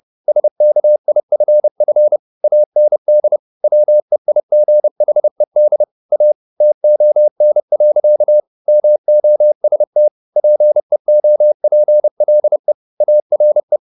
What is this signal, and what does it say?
Morse code, heard as a keyed tone, about 30 wpm, 600 Hz. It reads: SKIFF AND WEIGHED A TON. MOST PEOPLE ARE
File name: the old man and the sea / the old man and the sea 1 30wpm